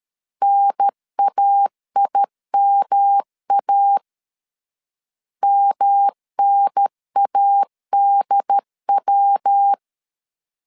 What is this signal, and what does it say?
Morse code, heard as a keyed tone, about 13 wpm, 800 Hz. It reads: NAIMA MNADW
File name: CW / DET / 1 3 1500C S C 12wpm 2750